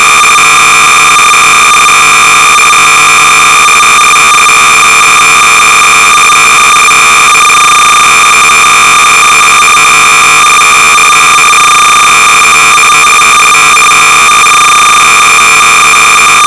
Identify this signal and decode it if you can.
Cassette data (Kansas City type).
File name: sorcerer / 300baud / biochart(v2)(modified by I.Gillies)-BASIC-Sorcerer-300baud